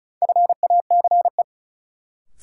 Morse code, heard as a keyed tone, about 35 wpm, 700 Hz. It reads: FACE
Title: Sentences from Top 300 Words - New Word Drill - 35wpm